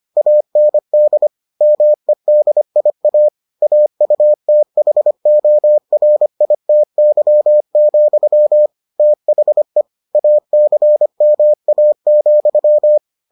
Morse code, heard as a keyed tone, about 25 wpm, 600 Hz. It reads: AND MEDIA AUTHORITY, THE ACMA,